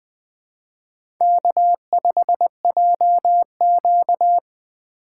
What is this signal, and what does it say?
Morse code, heard as a keyed tone, about 20 wpm, 700 Hz. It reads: K5JQ
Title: USA Call Signs - 4 Characters - 2x Character Spacing - 20wpm